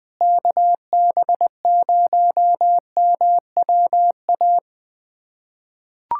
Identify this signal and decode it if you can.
Morse code, heard as a keyed tone, about 20 wpm, 700 Hz. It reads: KB0MWA
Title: USA Callsign DE Callsign - 6 Characters - 20wpm